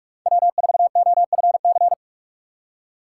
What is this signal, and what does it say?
Morse code, heard as a keyed tone, about 47 wpm, 700 Hz. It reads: W4OFC
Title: USA Call Signs - 6x Character Spacing - 45wpm